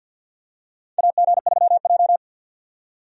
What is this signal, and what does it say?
Morse code, heard as a keyed tone, about 51 wpm, 700 Hz. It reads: AG3J
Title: USA Call Signs - 4 Characters - 50wpm